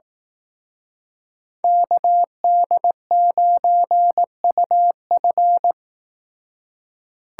Morse code, heard as a keyed tone, about 18 wpm, 700 Hz. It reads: KD9UF
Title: Speed Racing - USA 2x2 Call Signs 15wpm